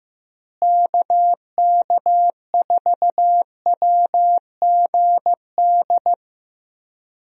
Morse code, heard as a keyed tone, about 16 wpm, 700 Hz. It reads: KK4WGD
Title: USA Call Signs - 6 Characters - Repeated 3x - 15wpm